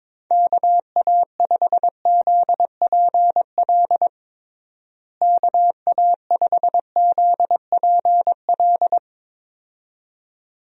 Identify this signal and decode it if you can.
Morse code, heard as a keyed tone, about 22 wpm, 700 Hz. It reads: KA5ZPL KA5ZPL
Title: USA Call Signs - 6 Characters - Repeated 2x - 22wpm